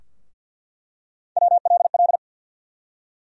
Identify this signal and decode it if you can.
Morse code, heard as a keyed tone, about 53 wpm, 700 Hz. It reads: WLL
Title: USA Call Sign Postfixes - 3 Letters 50wpm